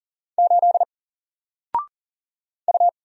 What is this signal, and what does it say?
Morse code, heard as a keyed tone, about 40 wpm, 700 Hz. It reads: 8 U
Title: Single Letter-Number Alt 40wpm